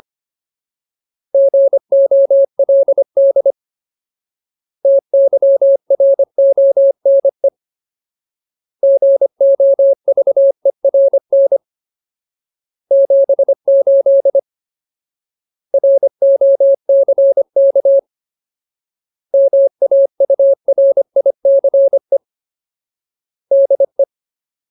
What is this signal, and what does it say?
Morse code, heard as a keyed tone, about 25 wpm, 550 Hz. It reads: GOLD TYRONE GOVERN 78 ROCK MAURICE DE
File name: Ultimate Word List / 25WPM Ultimate